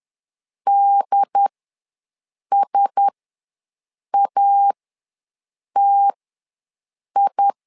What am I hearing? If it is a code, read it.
Morse code, heard as a keyed tone, about 10 wpm, spaced as for 6 wpm, 800 Hz. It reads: DSATI
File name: LEZIONI / 4 Q1S ws 10wpm 2208